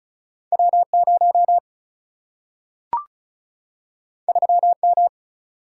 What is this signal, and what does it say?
Morse code, heard as a keyed tone, about 36 wpm, 700 Hz. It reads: W 0 3 M
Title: Single-Letter Single-Number 35wpm